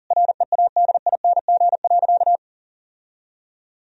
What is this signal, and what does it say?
Morse code, heard as a keyed tone, about 40 wpm, 700 Hz. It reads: READING.
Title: Sentences from Top 1000 Words - Review Part 3 - 40wpm